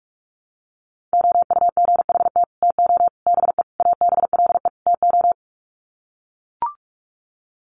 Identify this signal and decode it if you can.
Morse code, heard as a keyed tone, about 44 wpm, 700 Hz. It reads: OUGHT TO BE ABLE TO
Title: Sets of 5 Words Encore2 45wpm